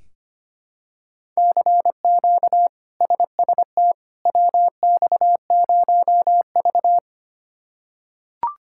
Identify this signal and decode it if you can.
Morse code, heard as a keyed tone, about 25 wpm, 700 Hz. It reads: CQ SST WX0V
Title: K1USN SST QSO Practice - 25wpm